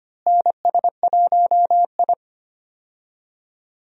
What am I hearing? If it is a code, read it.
Morse code, heard as a keyed tone, about 25 wpm, 700 Hz. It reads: NS1I
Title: USA Call Signs - 4 Characters - Repeated 3x - 25wpm